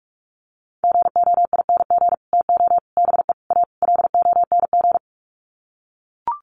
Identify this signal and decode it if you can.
Morse code, heard as a keyed tone, about 44 wpm, 700 Hz. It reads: GOING TO BE A LONG
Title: Sets of 5 Words Encore2 45wpm